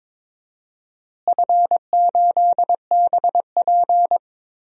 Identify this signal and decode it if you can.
Morse code, heard as a keyed tone, about 22 wpm, 700 Hz. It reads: F8BP
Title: DXCC-100 Call Signs - 4 Characters - Repeated 2x - 22wpm